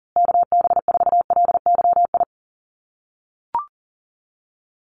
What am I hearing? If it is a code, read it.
Morse code, heard as a keyed tone, about 40 wpm, 700 Hz. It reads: KB4LYI
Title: USA Call Signs Repeated 2x 40wpm